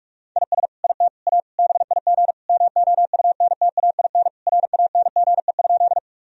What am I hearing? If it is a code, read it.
Morse code, heard as a keyed tone, about 41 wpm, 700 Hz. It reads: IS IT A BIG MOUNTAIN RANGE?
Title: Sentences from Top 1000 Words - Review Part 4 - 45wpm